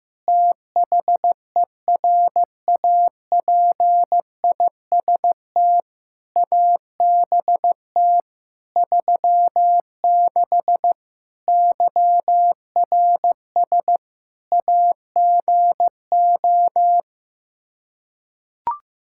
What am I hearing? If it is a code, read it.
Morse code, heard as a keyed tone, about 15 wpm, 700 Hz. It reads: THERAPIST ABT 36 YRS AGO
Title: Practice QSO - Exchange 3 - More Info - 15wpm